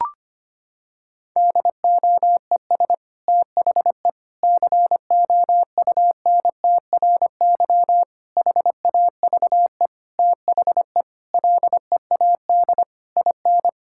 Morse code, heard as a keyed tone, about 25 wpm, 700 Hz. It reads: DOES THE COUNTRY HAVE THE LEAD IN
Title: Sentences from Top 1000 Words - Review Part 5 - 25wpm